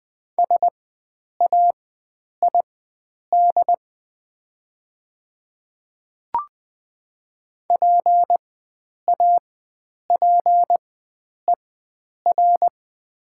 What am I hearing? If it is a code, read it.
Morse code, heard as a keyed tone, about 20 wpm, spaced as for 9 wpm, 700 Hz. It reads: SAID PAPER
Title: 4x Character Spacing - Top 300 Words - 20wpm